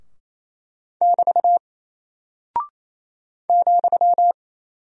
Morse code, heard as a keyed tone, about 28 wpm, 700 Hz. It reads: <BT> ,
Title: Single Character (Letter:Number:Punctuation:Prosign) Alt 28wpm